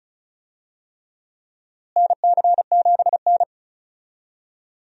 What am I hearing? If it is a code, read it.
Morse code, heard as a keyed tone, about 34 wpm, 700 Hz. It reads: NC7N
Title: USA Call Signs - 4 Characters - Repeated 3x - 35wpm